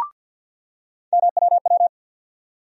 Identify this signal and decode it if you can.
Morse code, heard as a keyed tone, about 52 wpm, 700 Hz. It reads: MWW
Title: USA Call Sign Postfixes - 3 Letters 50wpm